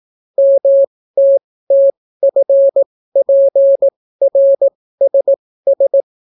Morse code, heard as a keyed tone, about 18 wpm, spaced as for 14 wpm, 550 Hz. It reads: MTTFPRSS